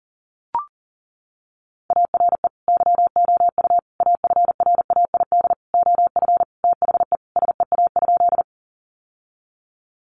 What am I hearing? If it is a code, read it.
Morse code, heard as a keyed tone, about 40 wpm, 700 Hz. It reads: ARE YOU AFRAID OF THE SEA?
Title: Sentences from Top 1500 Words 40wpm